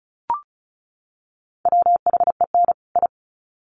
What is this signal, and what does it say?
Morse code, heard as a keyed tone, about 34 wpm, 700 Hz. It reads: WHEN I
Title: Sets of 2 Words 35wpm